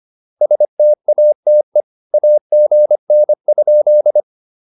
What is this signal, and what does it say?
Morse code, heard as a keyed tone, about 25 wpm, 590 Hz. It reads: STATE AGN?